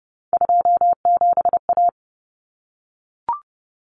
Morse code, heard as a keyed tone, about 30 wpm, 700 Hz. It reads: 27A
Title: Common Cut Numbers - 3-Digit 30wpm